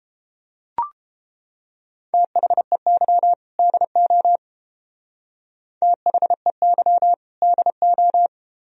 Morse code, heard as a keyed tone, about 30 wpm, 700 Hz. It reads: THEY DO THEY DO